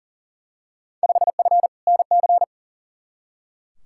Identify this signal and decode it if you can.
Morse code, heard as a keyed tone, about 40 wpm, 700 Hz. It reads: 5F NC
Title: Field Day Exchanges Encore - 40wpm